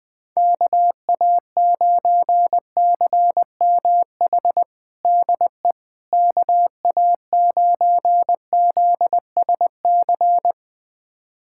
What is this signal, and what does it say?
Morse code, heard as a keyed tone, about 21 wpm, 700 Hz. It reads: KA9CMH DE KA9ZSC
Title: USA Callsign DE Callsign 20wpm